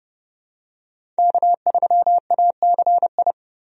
Morse code, heard as a keyed tone, about 31 wpm, 700 Hz. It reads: K3ACI
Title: USA Call Signs - 5 Characters - Repeated 2x - 30wpm